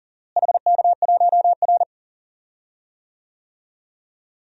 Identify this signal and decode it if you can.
Morse code, heard as a keyed tone, about 40 wpm, 700 Hz. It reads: HK1R
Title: DXCC-100 Call Signs - 4 Characters - Repeated 3x - 40wpm